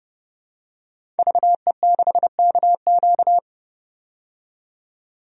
Morse code, heard as a keyed tone, about 30 wpm, 700 Hz. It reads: VE6KQ
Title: DXCC-100 Call Signs - 5 Characters - 30wpm